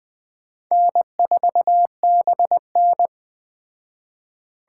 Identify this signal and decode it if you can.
Morse code, heard as a keyed tone, about 20 wpm, 700 Hz. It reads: N4BN
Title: Speed Racing - USA 1x2 Call Signs 17wpm